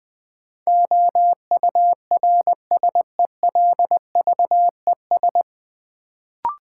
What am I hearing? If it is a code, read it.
Morse code, heard as a keyed tone, about 20 wpm, 700 Hz. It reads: OURSELVES
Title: Sentences from Top 1600 Words - New Word Drill - 20wpm